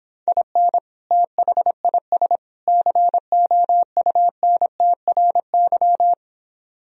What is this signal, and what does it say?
Morse code, heard as a keyed tone, about 26 wpm, 700 Hz. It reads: IN THIS COUNTRY